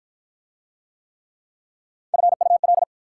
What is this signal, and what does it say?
Morse code, heard as a keyed tone, about 46 wpm, 690 Hz. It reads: FUL